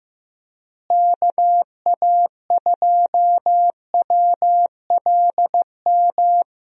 Morse code, heard as a keyed tone, about 15 wpm, 700 Hz. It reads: KA2WLM
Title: USA Call Signs - 6 Characters - Repeated 3x - 15wpm